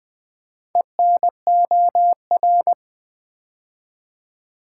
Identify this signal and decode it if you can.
Morse code, heard as a keyed tone, about 20 wpm, 700 Hz. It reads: ENOR
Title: Build A Word - 2 Letters At A Time - 8-Letter Words - 20wpm